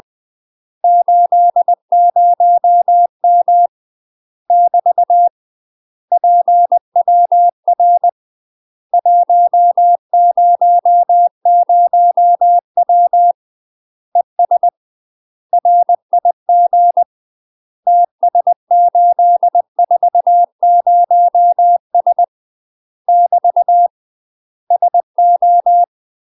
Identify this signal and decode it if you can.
Morse code, heard as a keyed tone, about 20 wpm, 700 Hz. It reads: 80M <BT> PWR 100W ES RIG TS840S <BT> SO
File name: UK MASTER QSOs TRAINING/20WPM